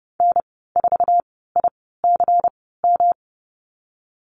Morse code, heard as a keyed tone, about 30 wpm, spaced as for 17 wpm, 700 Hz. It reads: N4ICM
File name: Lesson 19g - Letter w - Call Signs 30wpm